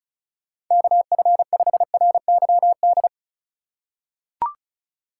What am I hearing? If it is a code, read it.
Morse code, heard as a keyed tone, about 34 wpm, 700 Hz. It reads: KF5RYD
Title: USA Call Signs - 6 Characters - 2x Character Spacing - 35wpm